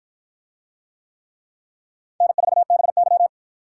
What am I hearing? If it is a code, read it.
Morse code, heard as a keyed tone, about 53 wpm, 700 Hz. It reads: N4BY